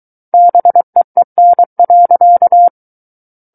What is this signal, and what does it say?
Morse code, heard as a keyed tone, about 23 wpm, 700 Hz. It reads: BEEN.
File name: Book One / 14-Chapter 14 - 23 WPM